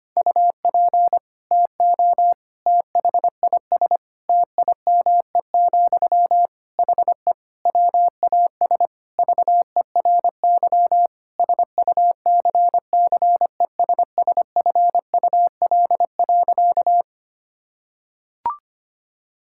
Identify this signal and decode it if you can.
Morse code, heard as a keyed tone, about 25 wpm, 700 Hz. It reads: UP TO THIS TIME, HE WAS VERY SUCCESSFUL.
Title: Sentences from Top 1000 Words - Review Part 6 - 25wpm